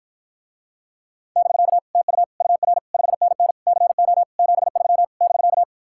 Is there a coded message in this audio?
Yes: Morse code from 1.4 to 5.6 s, heard as a keyed tone, about 52 wpm, 700 Hz. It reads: <BK> TU UR 5NN CO 73 <BK>